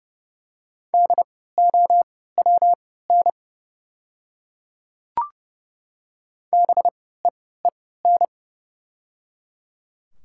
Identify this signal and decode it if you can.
Morse code, heard as a keyed tone, about 30 wpm, spaced as for 17 wpm, 700 Hz. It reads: DOWN BEEN
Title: 3x Character Spacing - Top 100 Words - 30wpm